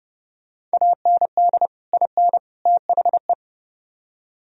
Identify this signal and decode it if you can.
Morse code, heard as a keyed tone, about 34 wpm, 700 Hz. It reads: AND IN THE